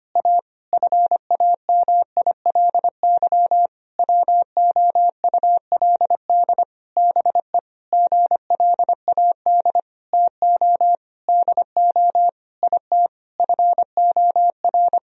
Morse code, heard as a keyed tone, about 25 wpm, 700 Hz. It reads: A FAMILY WOULD BE GLAD TO DO IT FOR
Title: Sentences from Top 1000 Words - Review Part 3 - 25wpm